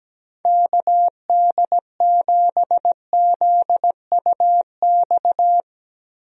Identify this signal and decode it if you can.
Morse code, heard as a keyed tone, about 17 wpm, 700 Hz. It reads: KD7ZUX
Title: USA Call Signs - 6 Characters - Repeated 2x - 17wpm